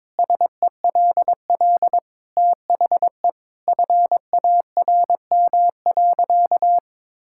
Morse code, heard as a keyed tone, about 22 wpm, 700 Hz. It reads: SELL THE FARM.
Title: Sentences from Top 1000 Words - Review Part 6 - 22wpm